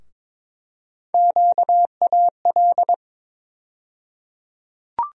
Morse code, heard as a keyed tone, about 22 wpm, 700 Hz. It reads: QAL
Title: USA Call Sign Postfixes - 3 Letters 22wpm